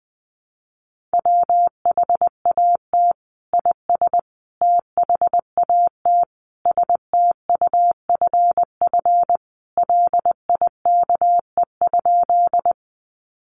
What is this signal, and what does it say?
Morse code, heard as a keyed tone, about 20 wpm, 700 Hz. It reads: WHAT IS THAT STUFF LIKE?
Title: Sentences from Top 1400 Words 20wpm